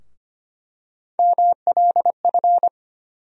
Morse code, heard as a keyed tone, about 25 wpm, 700 Hz. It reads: MLF
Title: USA Call Sign Postfixes - 3 Letters 25wpm